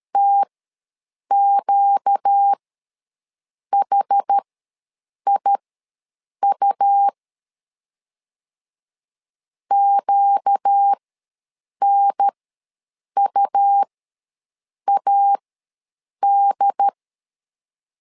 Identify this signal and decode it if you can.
Morse code, heard as a keyed tone, about 13 wpm, spaced as for 7 wpm, 800 Hz. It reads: TQHIU QNUAD
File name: LEZIONI / 4 Q ws 12wpm 1912